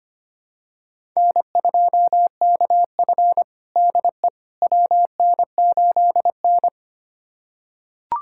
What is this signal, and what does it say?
Morse code, heard as a keyed tone, about 25 wpm, 700 Hz. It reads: N2KF DE WN8N
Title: USA Callsign DE Callsign - 4 Characters - 25wpm